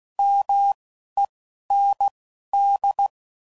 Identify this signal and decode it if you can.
Morse code, heard as a keyed tone, about 16 wpm, spaced as for 11 wpm, 800 Hz. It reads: MEND